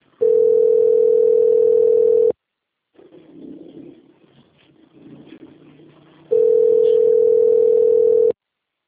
Ringback tone (North America), 440 + 480 Hz.